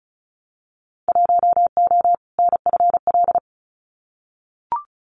Morse code, heard as a keyed tone, about 34 wpm, 700 Hz. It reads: 1O NFL
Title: Winter Field Day Exchanges 35wpm